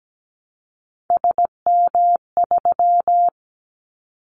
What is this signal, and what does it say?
Morse code, heard as a keyed tone, about 17 wpm, 700 Hz. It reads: SM3
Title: Callsign Prefixes for Top 25 DXCC - 17wpm